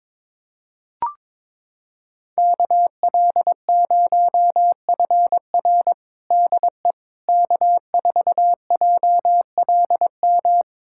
Morse code, heard as a keyed tone, about 22 wpm, 700 Hz. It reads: KL0FR DE K4JLM
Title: USA Callsign DE Callsign - 5 Characters - 22wpm